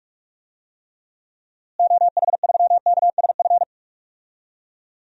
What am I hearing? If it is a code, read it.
Morse code, heard as a keyed tone, about 46 wpm, 700 Hz. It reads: OH3KSF
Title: DXCC-100 Call Signs - 6 Characters - Repeated 2x - 45wpm